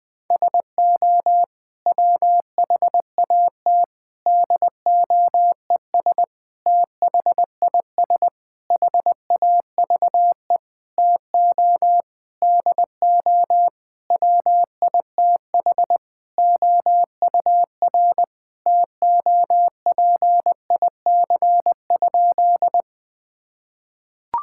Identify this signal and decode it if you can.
Morse code, heard as a keyed tone, about 20 wpm, 700 Hz. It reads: SO WHAT DOES THIS HAVE TO DO WITH OUR TOPIC?
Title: Sentences with 8-Word Phrases - Spoken Beginning - 20wpm